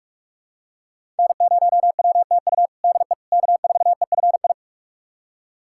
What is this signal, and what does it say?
Morse code, heard as a keyed tone, about 44 wpm, 700 Hz. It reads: N0WTU DE K4EFI
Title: USA Callsign DE Callsign - 5 Characters - 45wpm